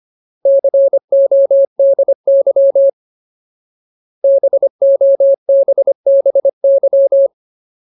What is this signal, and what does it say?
Morse code, heard as a keyed tone, about 25 wpm, 550 Hz. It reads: CODY BOBBY
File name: Ultimate Word List / 25WPM Ultimate